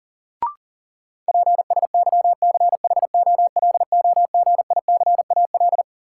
Morse code, heard as a keyed tone, about 40 wpm, 700 Hz. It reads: PSYCHOLOGICAL
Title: Sentences from Top 1400 Words - New Word Drill Alt - 40wpm